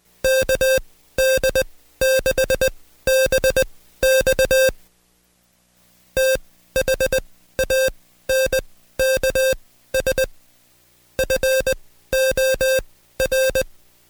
Morse code, heard as a keyed tone, about 20 wpm, spaced as for 14 wpm, 530 Hz. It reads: KD6BX THANKS FOR
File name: Code Quick / AudioDownld2 / Seg133-Stepper6 13WPM